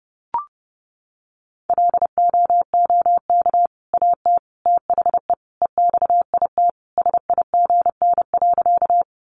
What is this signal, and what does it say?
Morse code, heard as a keyed tone, about 30 wpm, 700 Hz. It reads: LOOK AT THE EXIT SIGN.
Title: Sentences from Top 1100 Words 30wpm